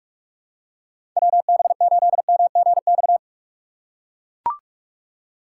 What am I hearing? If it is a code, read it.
Morse code, heard as a keyed tone, about 43 wpm, 700 Hz. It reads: WB8MGX
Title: USA Call Signs - 6 Characters - 4x Character Spacing - 45wpm